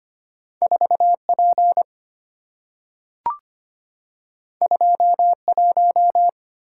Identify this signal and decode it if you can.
Morse code, heard as a keyed tone, about 25 wpm, 700 Hz. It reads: 4P 21
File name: Two Letter-Number Combinations - 25wpm